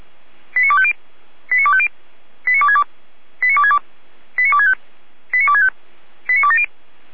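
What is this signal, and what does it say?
Five-tone selective call (ZVEI), heard as quick runs of five tones, tones of about 70 ms. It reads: 89250 89250 89262 89262 89265 89265 89270
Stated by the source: ZVEI five-tone selective call received on SDR.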